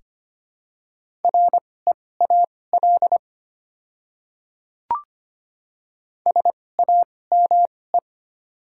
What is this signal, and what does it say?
Morse code, heard as a keyed tone, about 25 wpm, spaced as for 18 wpm, 690 Hz. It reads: REAL SAME
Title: 2x Character Spacing - Top 300 Words - 25wpm